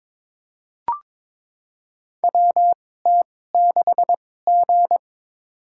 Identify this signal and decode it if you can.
Morse code, heard as a keyed tone, about 22 wpm, spaced as for 16 wpm, 700 Hz. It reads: WT6G
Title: USA Call Signs - 2x Character Spacing - 22wpm